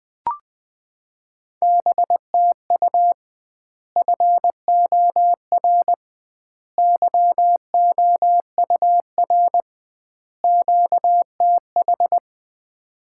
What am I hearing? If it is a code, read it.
Morse code, heard as a keyed tone, about 21 wpm, 700 Hz. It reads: BTU FOR YOUR QTH
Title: Common QSO Phrases - 2x Word-Spacing - 20wpm